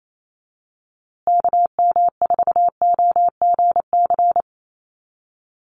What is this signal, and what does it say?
Morse code, heard as a keyed tone, about 28 wpm, 700 Hz. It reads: KM4OGC
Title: USA Call Signs Repeated 3x 28wpm